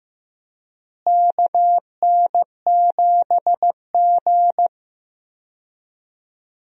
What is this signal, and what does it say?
Morse code, heard as a keyed tone, about 15 wpm, 700 Hz. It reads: KN7G
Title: USA Call Signs - 4 Characters - Repeated 2x - 15wpm